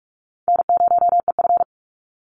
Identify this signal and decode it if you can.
Morse code, heard as a keyed tone, about 44 wpm, 700 Hz. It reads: N0EF
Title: Call Signs - 45wpm